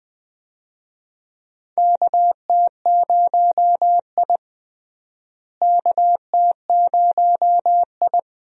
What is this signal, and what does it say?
Morse code, heard as a keyed tone, about 20 wpm, 700 Hz. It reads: KT0I KT0I
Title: USA Call Signs - 4 Characters - Repeated 2x - 20wpm